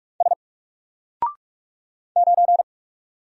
Morse code, heard as a keyed tone, about 43 wpm, 700 Hz. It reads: S 9
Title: Single Character (Letter:Number:Punctuation:Prosign) Alt 45wpm